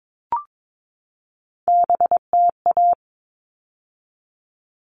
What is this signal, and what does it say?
Morse code, heard as a keyed tone, about 22 wpm, 700 Hz. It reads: BTA
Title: All Cut Numbers - 3-Digit 22wpm